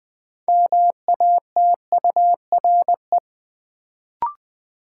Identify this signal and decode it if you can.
Morse code, heard as a keyed tone, about 20 wpm, 700 Hz. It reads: MATURE
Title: Build A Word - 2 Letters At A Time - 6-Letter Words - 20wpm